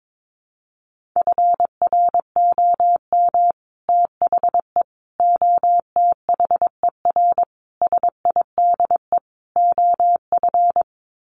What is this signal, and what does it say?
Morse code, heard as a keyed tone, about 22 wpm, 700 Hz. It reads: FROM THE OTHER SIDE OF
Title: Sets of 5 Words Encore 22wpm